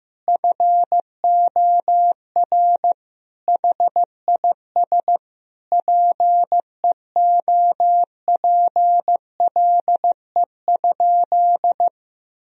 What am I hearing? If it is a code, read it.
Morse code, heard as a keyed tone, about 15 wpm, 700 Hz. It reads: FOR HIS PEOPLE?
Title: Sentences from Top 1000 Words - Review Part 3 - 15wpm